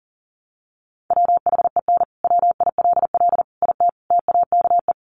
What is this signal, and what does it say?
Morse code, heard as a keyed tone, about 40 wpm, 700 Hz. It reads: WHEN WILL IT TAKE
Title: Sentences from Top 200 Words 40wpm